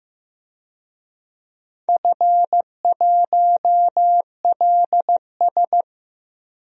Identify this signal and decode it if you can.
Morse code, heard as a keyed tone, about 15 wpm, 700 Hz. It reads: F1LS
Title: DXCC-100 Call Signs - 4 Characters - 15wpm